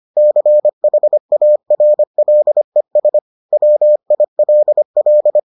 Morse code, heard as a keyed tone, about 25 wpm, 600 Hz. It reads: CHARLES WILL